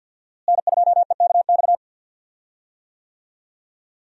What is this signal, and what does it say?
Morse code, heard as a keyed tone, about 52 wpm, 700 Hz. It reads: N2EKX
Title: USA Call Signs - 5 Characters - Repeated 2x - 50wpm